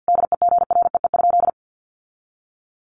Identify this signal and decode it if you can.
Morse code, heard as a keyed tone, about 50 wpm, 700 Hz. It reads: DEGREE?
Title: Sentences from Top 1500 Words 50wpm